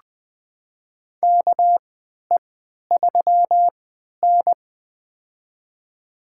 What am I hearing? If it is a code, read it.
Morse code, heard as a keyed tone, about 21 wpm, spaced as for 11 wpm, 700 Hz. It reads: KE3N
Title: USA Call Signs - 3x Character Spacing - 20wpm